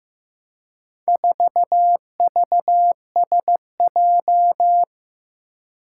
Morse code, heard as a keyed tone, about 15 wpm, 700 Hz. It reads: 4VSJ
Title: Four Challenging Letter-Number Combinations 15wpm